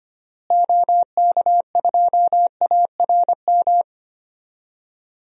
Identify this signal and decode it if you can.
Morse code, heard as a keyed tone, about 25 wpm, 700 Hz. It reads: OK2ARM
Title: DXCC-100 Call Signs - 6 Characters - Repeated 3x - 25wpm